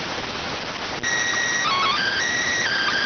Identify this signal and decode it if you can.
SSTV (slow-scan television), PD 290.